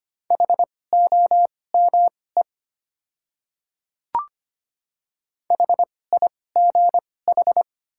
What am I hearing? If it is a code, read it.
Morse code, heard as a keyed tone, about 25 wpm, spaced as for 18 wpm, 690 Hz. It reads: HOME HIGH
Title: 2x Character Spacing - Top 200 Words - 25wpm